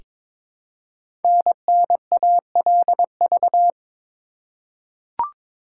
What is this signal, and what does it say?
Morse code, heard as a keyed tone, about 22 wpm, 700 Hz. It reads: NNALV
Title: Code Groups - 5 Letter:Numbers - 22wpm